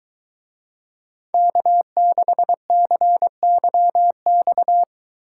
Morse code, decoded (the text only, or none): K6CYX